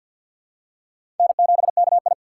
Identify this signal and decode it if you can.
Morse code, heard as a keyed tone, about 53 wpm, 700 Hz. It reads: N7KI
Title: USA Call Signs - 4 Characters - 3x Character Spacing - 50wpm